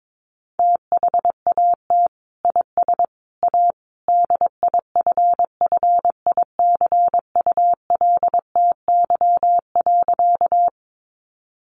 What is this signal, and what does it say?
Morse code, heard as a keyed tone, about 22 wpm, 700 Hz. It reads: THAT IS A DIFFICULTY.
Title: Sentences from Top 1500 Words 22wpm